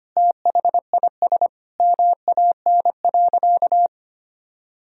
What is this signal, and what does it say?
Morse code, heard as a keyed tone, about 25 wpm, 700 Hz. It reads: THIS MAN.
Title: Sentences from Top 1000 Words - Review Part 1 - 25wpm